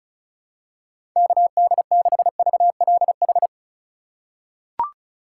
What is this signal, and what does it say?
Morse code, heard as a keyed tone, about 34 wpm, 700 Hz. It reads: KD6VLH
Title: USA Call Signs - 6 Characters - 3x Character Spacing - 35wpm